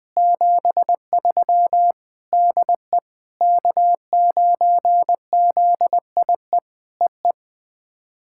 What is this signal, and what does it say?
Morse code, heard as a keyed tone, about 20 wpm, 700 Hz. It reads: 73 DE K9ZIE EE
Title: POTA QSO Practice - 20wpm